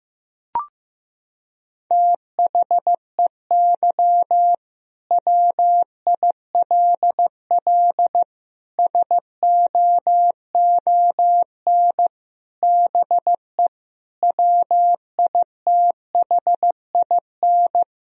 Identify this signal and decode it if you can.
Morse code, heard as a keyed tone, about 15 wpm, 700 Hz. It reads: THEY WILL SOON BE WITHIN
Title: Sentences from Top 1000 Words - Review Part 4 - 15wpm